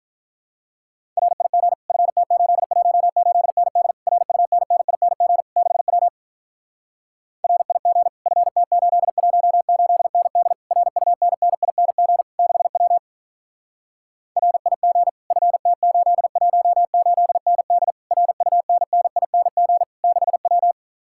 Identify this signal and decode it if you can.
Morse code, heard as a keyed tone, about 47 wpm, 700 Hz. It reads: RIG FT818ND RUNNING 6W RIG FT818ND RUNNING 6W RIG FT818ND RUNNING 6W